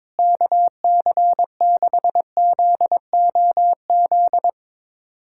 Morse code, heard as a keyed tone, about 22 wpm, 700 Hz. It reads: KC6ZOZ
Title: USA Callsign DE Callsign - 6 Characters - 22wpm